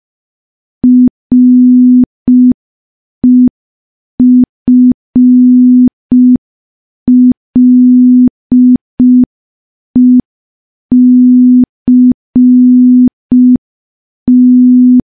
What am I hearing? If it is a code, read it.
Morse code, heard as a keyed tone, about 5 wpm, 250 Hz. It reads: REFLECT
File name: morse / morse-training-05WPM